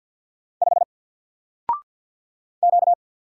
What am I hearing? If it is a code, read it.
Morse code, heard as a keyed tone, about 49 wpm, 700 Hz. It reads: 5 Q
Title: Single Character (Letter:Number:Punctuation:Prosign) Alt 50wpm